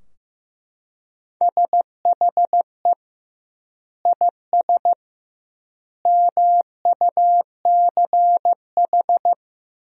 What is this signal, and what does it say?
Morse code, heard as a keyed tone, about 15 wpm, 700 Hz. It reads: SHE IS MUCH
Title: 2x Word-Spacing - Sentences from Top 400 Words - 15wpm